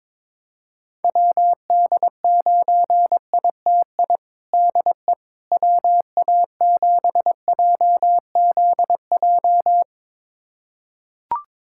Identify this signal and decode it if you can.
Morse code, heard as a keyed tone, about 22 wpm, 700 Hz. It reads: WD9ITI DE WA7JZJ
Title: USA Callsign DE Callsign - 6 Characters - 22wpm